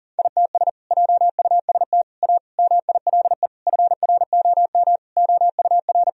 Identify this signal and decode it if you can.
Morse code, heard as a keyed tone, about 40 wpm, 700 Hz. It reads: ITS JUST A MILE FROM OUR
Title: Sentences from Top 1000 Words - Review Part 1 - 40wpm